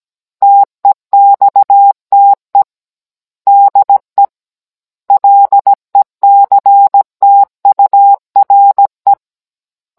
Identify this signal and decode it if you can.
Morse code, heard as a keyed tone, about 17 wpm, 800 Hz. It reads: TEXTE DE LECTURE
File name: Cours-NR18-16mm